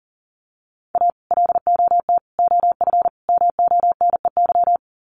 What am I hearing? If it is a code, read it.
Morse code, heard as a keyed tone, about 40 wpm, 700 Hz. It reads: A LOT OF MONEY
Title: Sets of 4 Words 40wpm